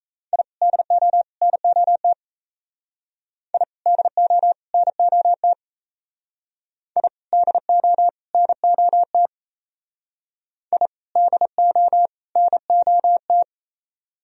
Morse code, heard as a keyed tone, about 35 wpm, 700 Hz. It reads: I DO NOT I DO NOT I DO NOT I DO NOT
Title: Speed-Racing - Sets of 3 Words 28wpm